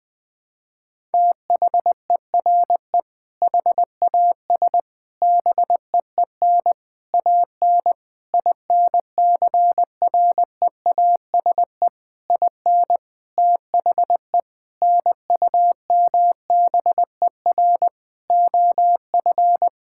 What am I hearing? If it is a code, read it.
Morse code, heard as a keyed tone, about 20 wpm, 700 Hz. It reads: THERE HAS BEEN AN INCREASE IN THE NUMBER OF
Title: Sentences with 6-Word Phrases - 20wpm